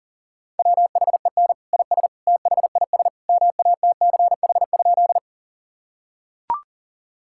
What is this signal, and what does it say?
Morse code, heard as a keyed tone, about 40 wpm, 700 Hz. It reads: WHEN IS THIS MATCH?
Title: Sentences from Top 1600 Words 40wpm